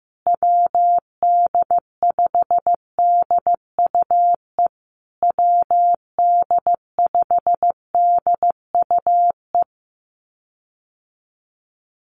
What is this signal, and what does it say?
Morse code, heard as a keyed tone, about 15 wpm, 700 Hz. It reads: WD5DUE WD5DUE
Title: USA Call Signs Repeated 3x 15wpm